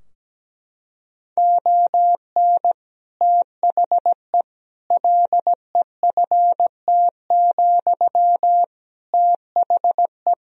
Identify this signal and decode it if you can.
Morse code, heard as a keyed tone, about 17 wpm, 700 Hz. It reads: ON THE LEFT, THE
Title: Sentences from Top 1000 Words - Review Part 4 - 17wpm